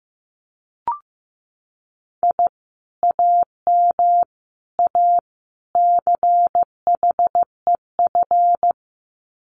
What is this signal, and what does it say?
Morse code, heard as a keyed tone, about 15 wpm, 700 Hz. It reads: I AM A CHEF
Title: Common QSO Phrases Encore 15wpm